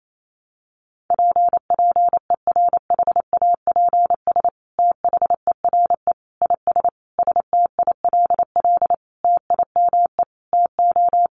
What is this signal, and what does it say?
Morse code, heard as a keyed tone, about 28 wpm, 700 Hz. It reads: P PERHAPS THERE IS STILL TIME TO
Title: Sentences from Top 1100 Words 28wpm